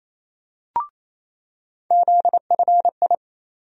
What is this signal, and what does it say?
Morse code, heard as a keyed tone, about 28 wpm, 700 Hz. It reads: ZFI
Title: USA Call Sign Postfixes - 3 Letters 28wpm